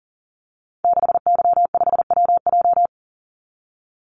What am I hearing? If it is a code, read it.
Morse code, heard as a keyed tone, about 40 wpm, 700 Hz. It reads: 6Y5WJ
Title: Call Signs Repeated 3x - 40wpm